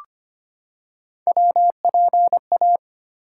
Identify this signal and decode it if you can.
Morse code, heard as a keyed tone, about 25 wpm, 700 Hz. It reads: WPA
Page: morse-code-ninja-practice-25wpm